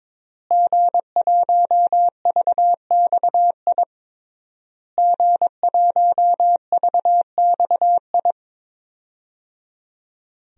Morse code, heard as a keyed tone, about 22 wpm, 700 Hz. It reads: G1VXI G1VXI
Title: DXCC-100 Call Signs - 5 Characters - Repeated 3x - 22wpm